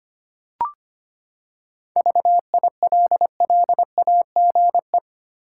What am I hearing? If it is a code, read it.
Morse code, heard as a keyed tone, about 25 wpm, 700 Hz. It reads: VILLAGE
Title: Seven Letter Words - 25wpm